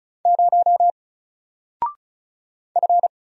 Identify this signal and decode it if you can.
Morse code, heard as a keyed tone, about 36 wpm, 700 Hz. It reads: 0 F
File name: Single-Letter-Number-35wpm